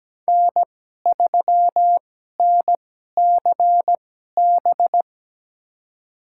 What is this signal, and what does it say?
Morse code, heard as a keyed tone, about 17 wpm, spaced as for 12 wpm, 700 Hz. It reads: N3NCB
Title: USA Call Signs - 5 Characters - 2x Character Spacing - 17wpm